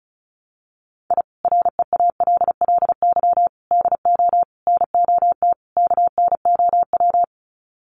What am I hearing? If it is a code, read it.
Morse code, heard as a keyed tone, about 34 wpm, 700 Hz. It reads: I REALLY DO NOT KNOW
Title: Sets of 5 Words Encore 35wpm